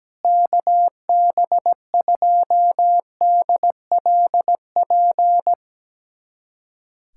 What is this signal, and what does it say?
Morse code, heard as a keyed tone, about 17 wpm, 700 Hz. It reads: KB2DLP
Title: USA Callsign DE Callsign - 6 Characters - 17wpm